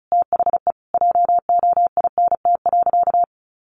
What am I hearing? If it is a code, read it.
Morse code, heard as a keyed tone, about 34 wpm, 700 Hz. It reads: THE JOINT.
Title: Sentences from Top 1300 Words 35wpm